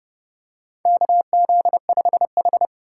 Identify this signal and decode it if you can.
Morse code, heard as a keyed tone, about 31 wpm, 700 Hz. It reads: KZ5H
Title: USA Call Signs - 4 Characters - 4x Character Spacing - 30wpm